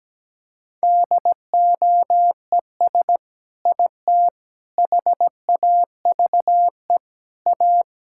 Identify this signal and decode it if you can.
Morse code, heard as a keyed tone, about 17 wpm, 700 Hz. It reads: DOES IT HAVE A
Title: Sentences from Top 1000 Words - Review Part 5 - 17wpm